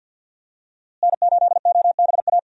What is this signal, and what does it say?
Morse code, heard as a keyed tone, about 46 wpm, 700 Hz. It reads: N8OBA
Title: USA Call Signs - 5 Characters - Repeated 3x - 50wpm